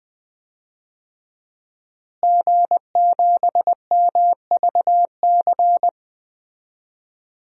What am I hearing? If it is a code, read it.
Morse code, heard as a keyed tone, about 20 wpm, 700 Hz. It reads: G7MVC
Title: DXCC-100 Call Signs - 5 Characters - Repeated 3x - 20wpm